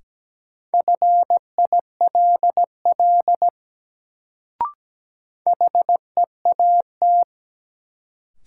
Morse code, heard as a keyed tone, about 17 wpm, 700 Hz. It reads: FILL HEAT
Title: Sentences from Top 1000 Words - Review Part 3 - New Word Drill - 17wpm